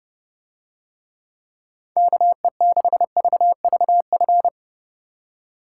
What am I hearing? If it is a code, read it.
Morse code, heard as a keyed tone, about 29 wpm, 700 Hz. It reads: KE6VVF